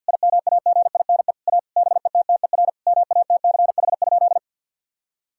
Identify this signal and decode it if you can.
Morse code, heard as a keyed tone, about 50 wpm, 700 Hz. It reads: IMAGINE A BETTER MATCH?
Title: Sentences from Top 1600 Words 50wpm